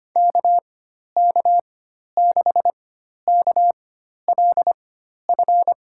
Morse code, heard as a keyed tone, about 25 wpm, spaced as for 12 wpm, 700 Hz. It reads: KK6KLF
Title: USA Call Signs - 6 Characters - 4x Character Spacing - 25wpm